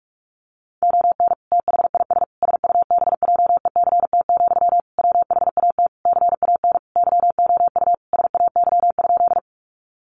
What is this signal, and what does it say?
Morse code, heard as a keyed tone, about 44 wpm, 700 Hz. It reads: ON THIS SUBJECT, WHAT CAN YOU SAY?
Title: Sentences from Top 800 Words 45wpm